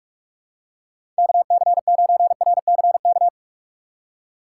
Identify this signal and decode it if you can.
Morse code, heard as a keyed tone, about 43 wpm, 700 Hz. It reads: KC9RCK
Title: USA Call Signs - 6 Characters - 6x Character Spacing - 45wpm